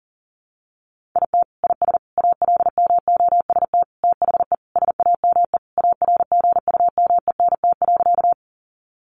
Morse code, heard as a keyed tone, about 40 wpm, 700 Hz. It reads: IT IS ALMOST THE SAME ARGUMENT.
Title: Sentences from Top 1400 Words 40wpm